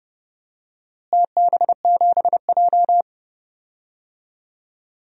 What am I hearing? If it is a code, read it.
Morse code, heard as a keyed tone, about 30 wpm, 700 Hz. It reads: TB7J
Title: DXCC-100 Call Signs - 4 Characters - Repeated 2x - 30wpm